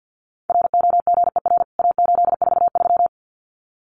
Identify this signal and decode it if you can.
Morse code, heard as a keyed tone, about 50 wpm, 700 Hz. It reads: ROGER A843
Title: CWT Exchange 50wpm